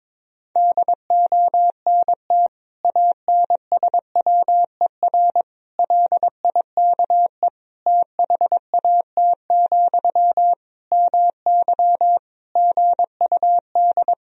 Morse code, heard as a keyed tone, about 22 wpm, 700 Hz. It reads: DONT ANSWER LIKE THAT, MY GUD
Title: Sentences from Top 400 Words and Abbreviations 22wpm